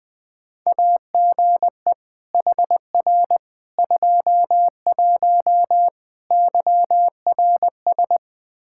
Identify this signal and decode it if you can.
Morse code, heard as a keyed tone, about 21 wpm, 700 Hz. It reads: AGE HR 21 YRS
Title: K1USN SST QSO Practice 20wpm